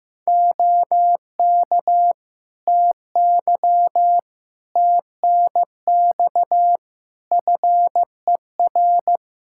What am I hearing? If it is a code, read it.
Morse code, heard as a keyed tone, about 15 wpm, 700 Hz. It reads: OK TY TNX FER
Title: K1USN SST QSO Practice 15wpm